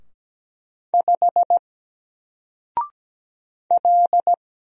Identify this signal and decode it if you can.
Morse code, heard as a keyed tone, about 17 wpm, 700 Hz. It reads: 5 L
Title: Single Letter-Number 17wpm